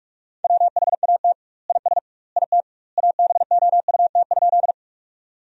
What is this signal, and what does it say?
Morse code, heard as a keyed tone, about 46 wpm, 700 Hz. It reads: WHAT IS IT ABOUT?
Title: 2x Word-Spacing - Sentences from Top 100 Words - 45wpm